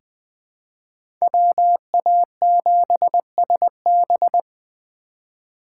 Morse code, heard as a keyed tone, about 20 wpm, 700 Hz. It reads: WA7SB
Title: 2023 SOTA Chaser Call Signs - 20wpm